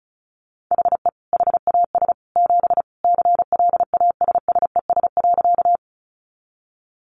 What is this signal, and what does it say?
Morse code, heard as a keyed tone, about 34 wpm, 700 Hz. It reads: HE HAS 7 CLASSES.
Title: Additional Easy Phrases 35wpm